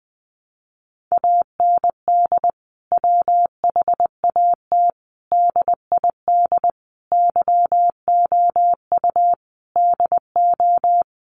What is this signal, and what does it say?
Morse code, heard as a keyed tone, about 20 wpm, 700 Hz. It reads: AND WHAT DID YOU DO
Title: Sentences from Top 300 Words 20wpm